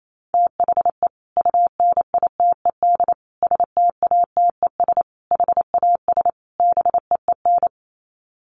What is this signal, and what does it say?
Morse code, heard as a keyed tone, about 28 wpm, 700 Hz. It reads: THE UNITED STATES HAS BEEN
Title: Sets of 5 Words Encore2 28wpm